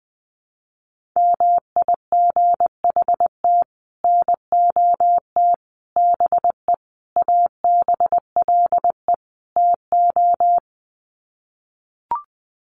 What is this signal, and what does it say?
Morse code, heard as a keyed tone, about 20 wpm, 700 Hz. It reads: MIGHT NOT BE ABLE TO
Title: Sets of 5 Words Encore2 20wpm